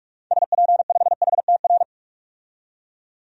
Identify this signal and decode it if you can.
Morse code, heard as a keyed tone, about 46 wpm, 700 Hz. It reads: SP5HTR
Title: DXCC-100 Call Signs - 6 Characters - 45wpm